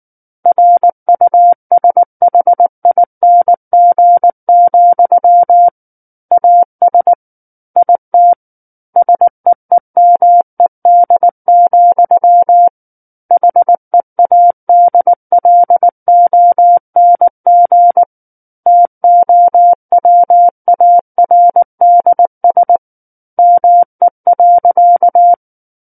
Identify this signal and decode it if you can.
Morse code, heard as a keyed tone, about 19 wpm, 700 Hz. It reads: RUSHING, AS IT SEEMED, HEADLONG TOWARDS ME.
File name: Book One / 10-Chapter 10 - 19 WPM